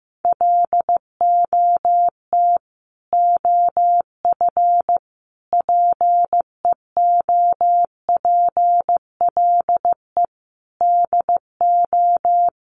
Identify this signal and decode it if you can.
Morse code, heard as a keyed tone, about 15 wpm, 700 Hz. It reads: LOT OF PEOPLE DO
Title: Sets of 5 Words Encore 15wpm